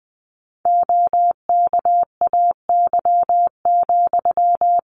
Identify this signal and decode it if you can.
Morse code, heard as a keyed tone, about 20 wpm, 700 Hz. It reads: OKAY,
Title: A Sound Of Thunder 20wpm